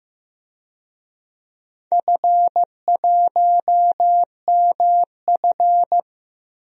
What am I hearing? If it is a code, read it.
Morse code, heard as a keyed tone, about 15 wpm, 700 Hz. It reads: F1MF
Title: DXCC-100 Call Signs - 4 Characters - 15wpm